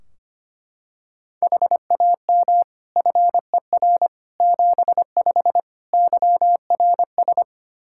Morse code, heard as a keyed tone, about 25 wpm, 700 Hz. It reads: HAM FER 75 YRS
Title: K1USN SST QSO Practice 25wpm